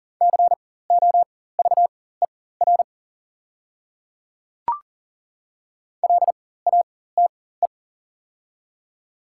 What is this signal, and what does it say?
Morse code, heard as a keyed tone, about 40 wpm, spaced as for 19 wpm, 700 Hz. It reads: COVER LATE
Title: 4x Character Spacing - Top 300 Words - 40wpm